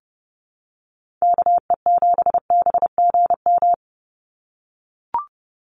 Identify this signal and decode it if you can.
Morse code, heard as a keyed tone, about 30 wpm, 700 Hz. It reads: KE7BGM
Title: NA SOTA Activator Callsigns 30wpm